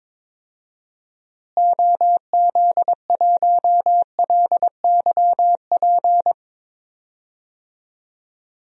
Morse code, heard as a keyed tone, about 22 wpm, 700 Hz. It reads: OZ1LYP
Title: DXCC-100 Call Signs - 6 Characters - Repeated 3x - 22wpm